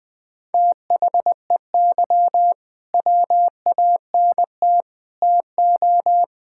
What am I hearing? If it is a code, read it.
Morse code, heard as a keyed tone, about 20 wpm, 700 Hz. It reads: THEY WANT TO
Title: Build A Phrase - 3-Word Phrases - 20wpm